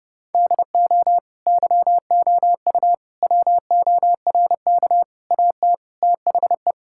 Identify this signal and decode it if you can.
Morse code, heard as a keyed tone, about 30 wpm, 700 Hz. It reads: DO YOU WORK AT THE
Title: Sentences from Top 1000 Words - Review Part 5 - 30wpm